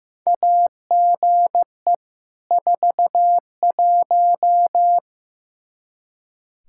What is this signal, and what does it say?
Morse code, heard as a keyed tone, about 15 wpm, 700 Hz. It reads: AGE 41
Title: Practice QSO - Exchange 3 - More Info - 15wpm